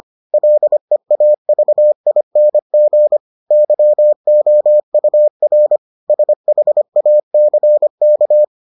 Morse code, heard as a keyed tone, about 25 wpm, 600 Hz. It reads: LEAVING YOUR SHACK